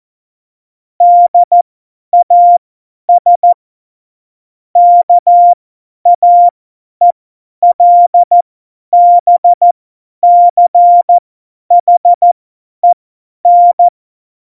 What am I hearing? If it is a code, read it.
Morse code, heard as a keyed tone, about 14 wpm, spaced as for 10 wpm, 700 Hz. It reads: DAS KAELBCHEN